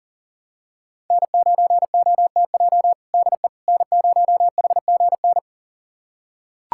Morse code, heard as a keyed tone, about 40 wpm, 700 Hz. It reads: N9OTJ DE N0HGN
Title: USA Callsign DE Callsign - 5 Characters - 40wpm